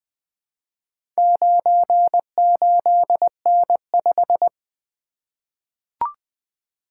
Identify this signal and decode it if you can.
Morse code, heard as a keyed tone, about 20 wpm, 700 Hz. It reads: 98N5
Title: Four Letter-Number Combinations - 20wpm